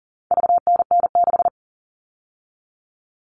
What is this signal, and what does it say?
Morse code, heard as a keyed tone, about 40 wpm, 700 Hz. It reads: VNN6
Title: All Cut Numbers - 4-Digit 40wpm